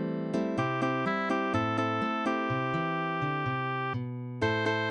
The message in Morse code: none